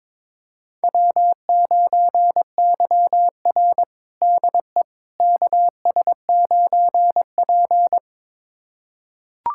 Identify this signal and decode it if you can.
Morse code, heard as a keyed tone, about 22 wpm, 700 Hz. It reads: W9YR DE KS9P
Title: USA Callsign DE Callsign - 4 Characters - 22wpm